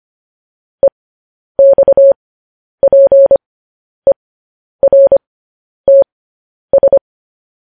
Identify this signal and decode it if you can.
Morse code, heard as a keyed tone, about 25 wpm, spaced as for 10 wpm, 550 Hz. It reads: EXPERTS
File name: podcast / News headlines-1x-25x10